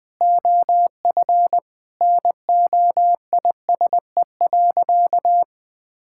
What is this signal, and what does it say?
Morse code, heard as a keyed tone, about 21 wpm, 700 Hz. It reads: OF NOISE.
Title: Sentences with 2-Word Phrases - 20wpm